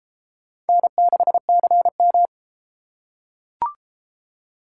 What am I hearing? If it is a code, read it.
Morse code, heard as a keyed tone, about 31 wpm, 700 Hz. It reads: N6CM